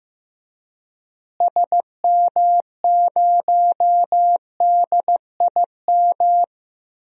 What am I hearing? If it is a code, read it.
Morse code, heard as a keyed tone, about 15 wpm, 700 Hz. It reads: SM0DIM
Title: DXCC-100 Call Signs - 6 Characters - Repeated 2x - 15wpm